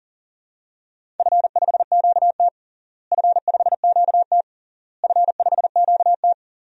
Morse code, heard as a keyed tone, about 40 wpm, 700 Hz. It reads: F5QT F5QT F5QT
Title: DXCC-100 Call Signs - 4 Characters - Repeated 3x - 40wpm